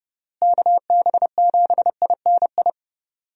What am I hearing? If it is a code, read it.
Morse code, heard as a keyed tone, about 30 wpm, 700 Hz. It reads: KB7INI
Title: USA Call Signs - 6 Characters - Repeated 2x - 30wpm